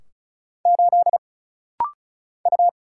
Morse code, heard as a keyed tone, about 34 wpm, 700 Hz. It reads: 8 U
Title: Single Letter-Number - Rapid-Fire 35wpm